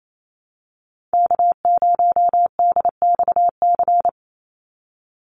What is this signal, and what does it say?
Morse code, heard as a keyed tone, about 28 wpm, 700 Hz. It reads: K0DXC
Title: Call Signs Repeated 2x - 28wpm